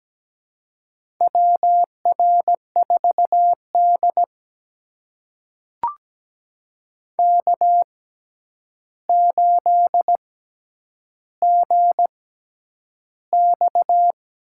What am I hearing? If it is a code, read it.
Morse code, heard as a keyed tone, about 17 wpm, spaced as for 9 wpm, 700 Hz. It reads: WR4D K8GX
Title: USA Call Signs - 4 Characters - 6x Character Spacing - 17wpm